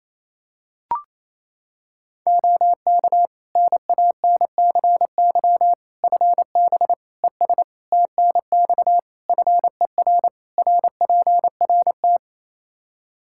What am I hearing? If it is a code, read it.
Morse code, heard as a keyed tone, about 28 wpm, 700 Hz. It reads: OK NANCY FB ES TNX FER RPRT
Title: Practice QSO - Exchange 2 - Info - 28wpm